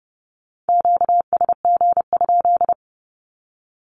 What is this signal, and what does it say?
Morse code, heard as a keyed tone, about 30 wpm, 700 Hz. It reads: QSG?
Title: Q-Codes 30wpm